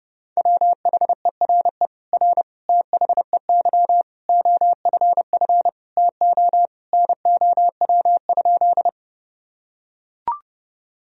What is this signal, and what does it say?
Morse code, heard as a keyed tone, about 30 wpm, 700 Hz. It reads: WHERE R THEY OFF TO NOW?
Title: Sentences from Top 200 Words and Abbreviations 30wpm